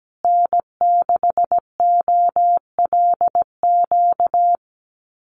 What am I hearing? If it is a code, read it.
Morse code, heard as a keyed tone, about 17 wpm, 700 Hz. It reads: N6OLQ
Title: USA Call Signs Repeated 3x 17wpm